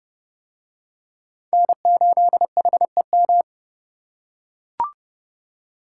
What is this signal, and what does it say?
Morse code, heard as a keyed tone, about 28 wpm, 700 Hz. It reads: N8HEM